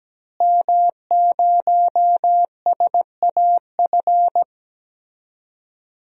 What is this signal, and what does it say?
Morse code, heard as a keyed tone, about 17 wpm, 700 Hz. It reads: M0SAF
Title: DXCC-100 Call Signs - 5 Characters - Repeated 2x - 17wpm